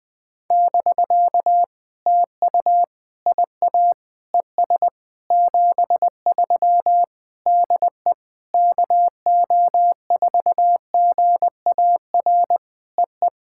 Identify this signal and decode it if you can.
Morse code, heard as a keyed tone, about 20 wpm, 700 Hz. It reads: <BK> TU IA ES 73 DE KO4GAR EE
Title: POTA QSO Practice - 20wpm